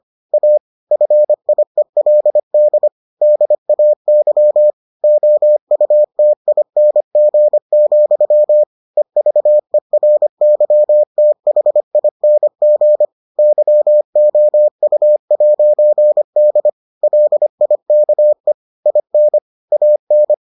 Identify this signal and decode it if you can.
Morse code, heard as a keyed tone, about 25 wpm, 600 Hz. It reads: A FIELD DAY OUTING, EVERYTHING YOU'D LIKE IN AN